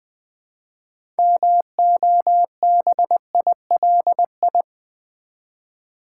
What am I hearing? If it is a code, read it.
Morse code, heard as a keyed tone, about 20 wpm, 700 Hz. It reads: MOBILI
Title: Build A Word - 2 Letters At A Time - 8-Letter Words - 20wpm